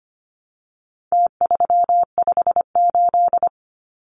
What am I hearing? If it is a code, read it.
Morse code, heard as a keyed tone, about 25 wpm, 700 Hz. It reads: T358
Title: Numbers - Contest Style - 4-Digit 25wpm